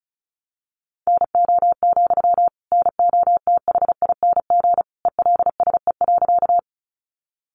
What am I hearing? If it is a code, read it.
Morse code, heard as a keyed tone, about 34 wpm, 700 Hz. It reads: NO, NOTHING ELSE.
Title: Sentences from Top 800 Words 35wpm